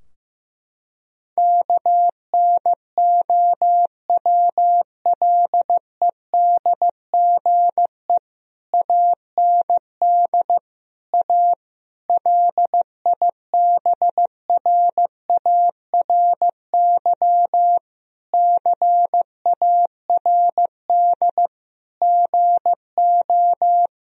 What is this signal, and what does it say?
Morse code, heard as a keyed tone, about 15 wpm, 700 Hz. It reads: KNOWLEDGE AND A LIBRARY CARD GO
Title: Sentences from Top 1000 Words - Review Part 4 - 15wpm